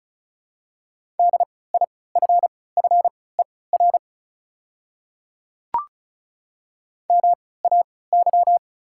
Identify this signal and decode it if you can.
Morse code, heard as a keyed tone, about 36 wpm, spaced as for 20 wpm, 690 Hz. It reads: DIFFER MAY
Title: 3x Character Spacing - Top 300 Words - 35wpm